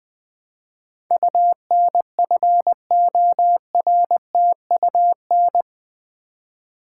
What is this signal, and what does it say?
Morse code, heard as a keyed tone, about 20 wpm, 700 Hz. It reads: UNFORTUN
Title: Build A Word - 4 Letters At A Time - 11-Letter Words - 20wpm